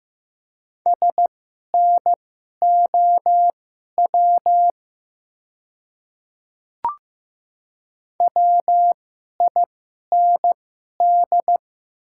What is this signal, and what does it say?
Morse code, heard as a keyed tone, about 15 wpm, spaced as for 11 wpm, 700 Hz. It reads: SNOW WIND
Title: 2x Character Spacing - Top 500 Words - 15wpm